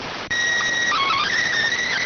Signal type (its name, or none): SSTV (slow-scan television)